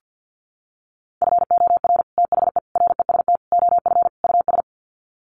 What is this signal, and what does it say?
Morse code, heard as a keyed tone, about 50 wpm, 700 Hz. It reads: FOR THE REST OF US
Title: Sets of 5 Words Encore 50wpm